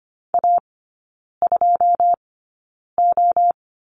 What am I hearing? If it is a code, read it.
Morse code, heard as a keyed tone, about 25 wpm, spaced as for 9 wpm, 700 Hz. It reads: A2O
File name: course / 25:10wpm / 093 Lesson 16e - Letter c - Three Characters 25wpm